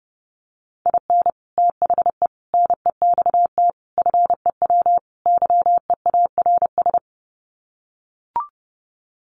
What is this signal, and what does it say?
Morse code, heard as a keyed tone, about 28 wpm, 700 Hz. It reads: IN THE NEXT FEW YEARS
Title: Sets of 5 Words Encore 30wpm